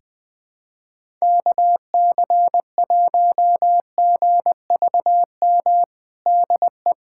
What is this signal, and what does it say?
Morse code, heard as a keyed tone, about 20 wpm, 700 Hz. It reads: KC1GVM DE
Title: USA Callsign DE Callsign 20wpm